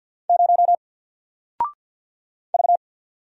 Morse code, heard as a keyed tone, about 48 wpm, 700 Hz. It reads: 0 V
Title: Single Character (Letter:Number:Punctuation:Prosign) 50wpm